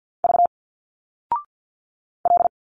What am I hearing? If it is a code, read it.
Morse code, heard as a keyed tone, about 50 wpm, 700 Hz. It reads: V L
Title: Single Letters 50wpm